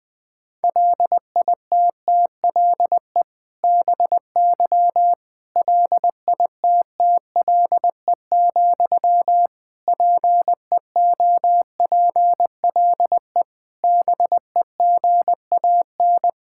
Morse code, heard as a keyed tone, about 21 wpm, 700 Hz. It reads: LITTLE BY LITTLE, PEOPLE BEGAN
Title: Sentences from Top 1600 Words 20wpm